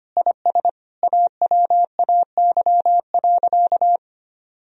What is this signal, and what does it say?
Morse code, heard as a keyed tone, about 25 wpm, 700 Hz. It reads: IS AWAY.
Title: Sentences from Top 1000 Words - Review Part 2 - 25wpm